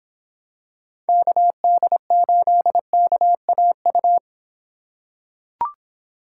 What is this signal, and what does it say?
Morse code, heard as a keyed tone, about 26 wpm, 700 Hz. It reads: KD8KAU